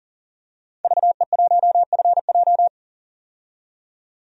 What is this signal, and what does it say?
Morse code, heard as a keyed tone, about 40 wpm, 700 Hz. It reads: VE1FJ
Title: DXCC-100 Call Signs - 5 Characters - 40wpm